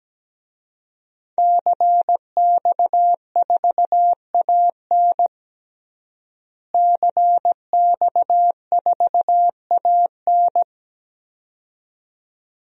Morse code, heard as a keyed tone, about 17 wpm, 700 Hz. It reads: CX4AN CX4AN
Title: DXCC-100 Call Signs - 5 Characters - Repeated 2x - 17wpm